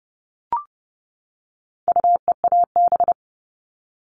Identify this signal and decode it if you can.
Morse code, heard as a keyed tone, about 28 wpm, 700 Hz. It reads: UEAB
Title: All Cut Numbers - 4-Digit 30wpm